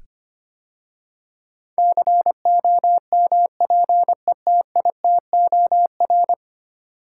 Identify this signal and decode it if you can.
Morse code, heard as a keyed tone, about 25 wpm, 690 Hz. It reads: COMPETITOR
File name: 10 Letter Words - First 3 Letters Given - 25wpm